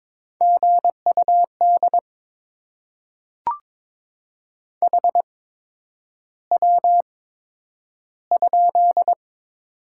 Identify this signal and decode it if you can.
Morse code, heard as a keyed tone, about 22 wpm, spaced as for 9 wpm, 700 Hz. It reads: GUD HW?
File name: Top 20 QSO Elements - 8x Character Spacing - 22wpm